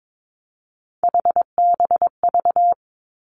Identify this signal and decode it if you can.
Morse code, heard as a keyed tone, about 22 wpm, 700 Hz. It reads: HBV
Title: Three Challenging Letter-Number Combinations 22wpm